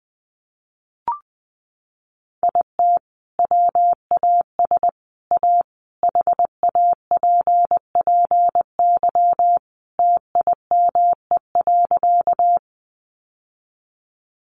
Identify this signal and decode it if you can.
Morse code, heard as a keyed tone, about 20 wpm, 700 Hz. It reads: IT WAS A HAPPY TIME.
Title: Sentences from Top 1000 Words 20wpm